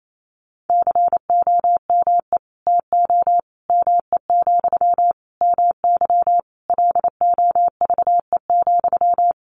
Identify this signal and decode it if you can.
Morse code, heard as a keyed tone, about 28 wpm, 700 Hz. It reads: COME TO ME, MY LOVE,
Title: Sentences from Top 1500 Words 28wpm